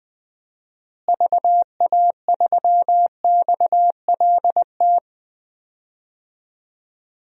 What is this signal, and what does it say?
Morse code, heard as a keyed tone, about 21 wpm, 700 Hz. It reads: VA3XLT
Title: DXCC-100 Call Signs - 6 Characters - Repeated 3x - 20wpm